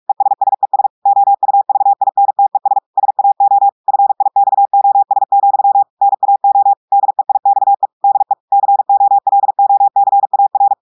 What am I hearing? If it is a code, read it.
Morse code, heard as a keyed tone, about 46 wpm, 800 Hz. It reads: ESSES OUVINTES SAO FIXOS, NAO DEIXE DE COLOCAR
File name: audios / portugais / portugais texte en clair 2 vitesse 45 mots minutes